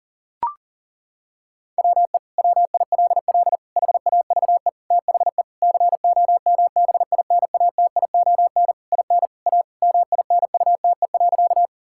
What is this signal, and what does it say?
Morse code, heard as a keyed tone, about 40 wpm, 700 Hz. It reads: WE WILL HAVE THE COMBINATION IN A MINUTE.
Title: Sentences from Top 1000 Words - Review Part 6 - 40wpm